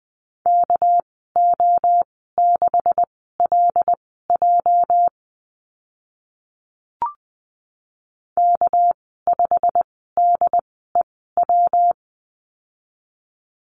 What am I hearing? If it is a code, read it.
Morse code, heard as a keyed tone, about 20 wpm, spaced as for 14 wpm, 700 Hz. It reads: KO6LJ K5DEW
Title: 213 Lesson 34g - Number 8 - Call Signs 20wpm